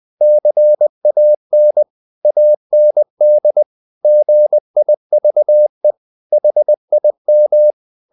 Morse code, heard as a keyed tone, about 20 wpm, 600 Hz. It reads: CAN AND GIVE HIM
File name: the old man and the sea / the old man and the sea part 1 20wpm